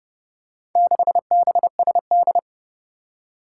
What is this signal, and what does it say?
Morse code, heard as a keyed tone, about 30 wpm, 700 Hz. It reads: 6BSD
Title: Four Challenging Letter-Number Combinations 30wpm